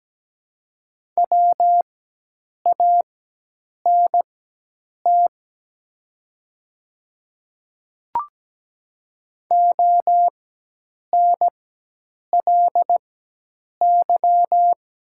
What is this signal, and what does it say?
Morse code, heard as a keyed tone, about 17 wpm, spaced as for 8 wpm, 700 Hz. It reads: WANT ONLY
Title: 4x Character Spacing - Top 200 Words - 17wpm